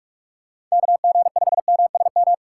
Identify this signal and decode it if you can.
Morse code, heard as a keyed tone, about 44 wpm, 700 Hz. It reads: KG5MSM
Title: USA Call Signs - 6 Characters - 3x Character Spacing - 45wpm